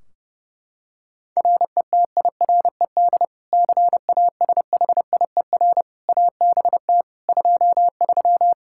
Morse code, heard as a keyed tone, about 30 wpm, 700 Hz. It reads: RETIRED CASHIER ABT 23
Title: Practice QSO - Exchange 3 - More Info - 30wpm